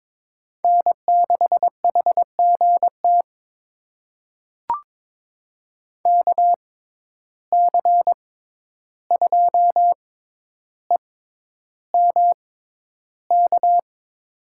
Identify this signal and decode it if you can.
Morse code, heard as a keyed tone, about 21 wpm, spaced as for 11 wpm, 700 Hz. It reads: N6HGT KC2EMK
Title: USA Call Signs - 6x Character Spacing - 22wpm